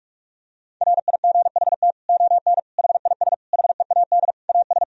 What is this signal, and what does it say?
Morse code, heard as a keyed tone, about 44 wpm, 700 Hz. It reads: RIGHT ON HIS HEAD AS
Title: Sentences from Top 1000 Words - Review Part 2 - 45wpm